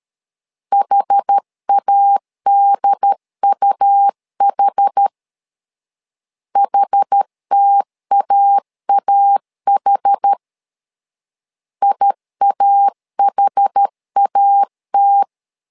Morse code, heard as a keyed tone, about 13 wpm, 800 Hz. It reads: HADUH HTAAH IAHAT
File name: LEZIONI / 2 UHDN / 2 UHD 12wpm 1540